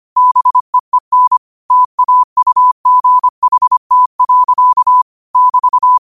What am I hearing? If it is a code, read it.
Morse code, heard as a keyed tone, about 25 wpm, 1000 Hz. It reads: BEEN TAUGHT. <BT>
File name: qotdfiles / 25wpm